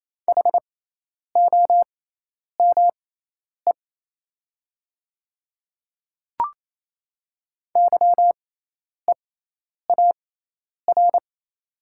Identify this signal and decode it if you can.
Morse code, heard as a keyed tone, about 28 wpm, spaced as for 10 wpm, 700 Hz. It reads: HOME YEAR
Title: 6x Character Spacing - Top 200 Words - 28wpm